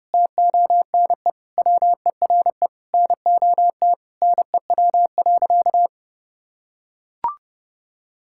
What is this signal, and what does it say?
Morse code, heard as a keyed tone, about 30 wpm, 700 Hz. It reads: TONE WERE NOT NEW.
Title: Sentences from Top 1600 Words 30wpm